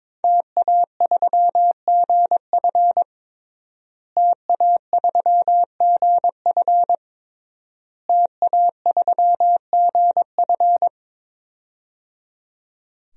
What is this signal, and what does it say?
Morse code, heard as a keyed tone, about 22 wpm, 700 Hz. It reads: TA3GF TA3GF TA3GF
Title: DXCC-100 Call Signs - 5 Characters - Repeated 3x - 22wpm